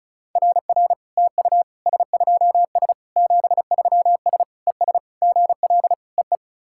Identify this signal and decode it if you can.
Morse code, heard as a keyed tone, about 36 wpm, 700 Hz. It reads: RR TU S2S 73S ES GL EE
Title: SOTA QSO Practice 35wpm